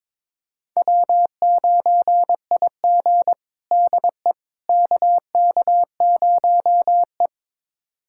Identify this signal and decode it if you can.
Morse code, heard as a keyed tone, about 22 wpm, 700 Hz. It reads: W9IG DE KK0E
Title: USA Callsign DE Callsign - 4 Characters - 22wpm